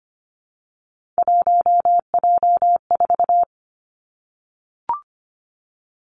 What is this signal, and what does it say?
Morse code, heard as a keyed tone, about 25 wpm, 700 Hz. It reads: 1J4
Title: Three Challenging Letter-Number Combinations 25wpm